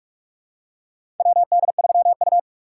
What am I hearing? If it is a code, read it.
Morse code, heard as a keyed tone, about 44 wpm, 700 Hz. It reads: WD3U
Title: USA Call Signs - 4 Characters - 3x Character Spacing - 45wpm